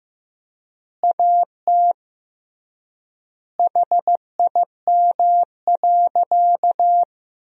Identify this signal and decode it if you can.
Morse code, heard as a keyed tone, about 15 wpm, 700 Hz. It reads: AT HIM.
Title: 3x Word-Spacing - Sentences from Top 100 Words - 15wpm